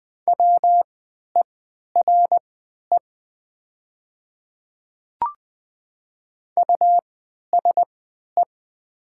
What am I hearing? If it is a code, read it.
Morse code, heard as a keyed tone, about 20 wpm, spaced as for 11 wpm, 700 Hz. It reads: WERE USE
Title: 3x Character Spacing - Top 100 Words - 20wpm